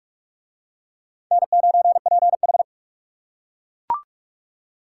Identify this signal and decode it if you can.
Morse code, heard as a keyed tone, about 39 wpm, 700 Hz. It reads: N9PH